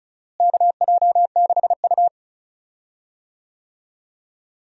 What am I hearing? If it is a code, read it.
Morse code, heard as a keyed tone, about 36 wpm, 700 Hz. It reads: KJ6U
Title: USA Call Signs - 4 Characters - Repeated 2x - 35wpm